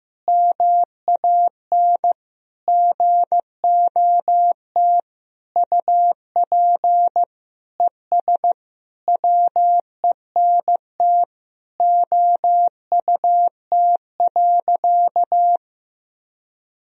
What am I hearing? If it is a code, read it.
Morse code, heard as a keyed tone, about 15 wpm, 700 Hz. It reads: MAN GOT UP ES WENT OUT.
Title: Sentences from Top 300 Words and Abbreviations 15wpm